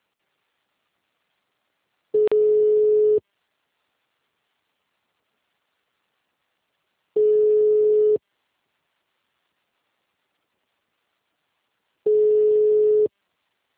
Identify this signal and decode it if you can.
Ringback tone (Europe, 425 Hz).